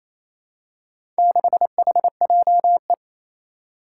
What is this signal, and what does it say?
Morse code, heard as a keyed tone, about 29 wpm, 700 Hz. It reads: 6HJE
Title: Four Challenging Letter-Number Combinations 28wpm